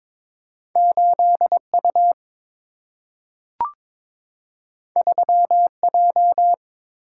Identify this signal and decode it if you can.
Morse code, heard as a keyed tone, about 22 wpm, 700 Hz. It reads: 8U 3J
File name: practice / 22wpm / Two Letter-Number Combinations - 22wpm